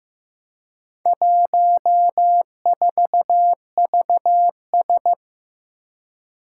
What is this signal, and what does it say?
Morse code, heard as a keyed tone, about 15 wpm, 700 Hz. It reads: 14VS
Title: Four Challenging Letter-Number Combinations 15wpm